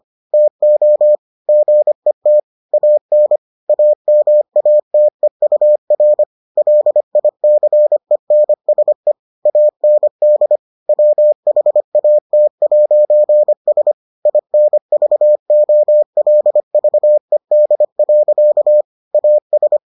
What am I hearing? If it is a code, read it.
Morse code, heard as a keyed tone, about 25 wpm, 600 Hz. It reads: TO GET AN AMATEUR LICENSE AND WHAT'S INVOLVED. AS